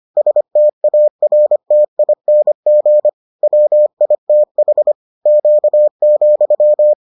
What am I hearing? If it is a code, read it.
Morse code, heard as a keyed tone, about 25 wpm, 600 Hz. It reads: STARTING WITH Q,